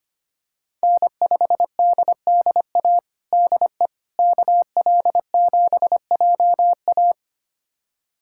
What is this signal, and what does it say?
Morse code, heard as a keyed tone, about 25 wpm, 700 Hz. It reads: N5DDA DE KL7JA
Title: USA Callsign DE Callsign - 5 Characters - 25wpm